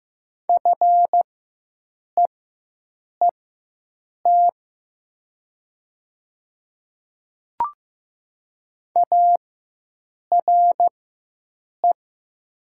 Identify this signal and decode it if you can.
Morse code, heard as a keyed tone, about 15 wpm, spaced as for 7 wpm, 700 Hz. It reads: FEET ARE
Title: 4x Character Spacing - Top 300 Words - 15wpm